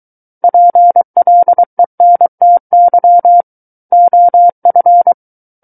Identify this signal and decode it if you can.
Morse code, heard as a keyed tone, about 23 wpm, 700 Hz. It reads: PLENTY OF
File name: Book One / 14-Chapter 14 - 23 WPM